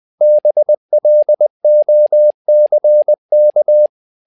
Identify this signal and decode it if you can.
Morse code, heard as a keyed tone, about 20 wpm, 600 Hz. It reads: BLOCK
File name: learning center / tower trouble / tower trouble 20wpm